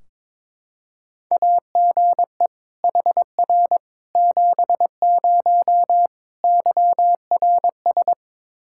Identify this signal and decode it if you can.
Morse code, heard as a keyed tone, about 22 wpm, 700 Hz. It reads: AGE HR 70 YRS
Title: Practice QSO - Exchange 3 - More Info - 22wpm